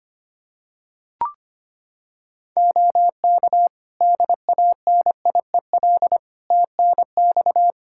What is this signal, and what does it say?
Morse code, heard as a keyed tone, about 25 wpm, 700 Hz. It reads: OK DANIEL TNX
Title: Practice QSO - Exchange 4 - Ending - 25wpm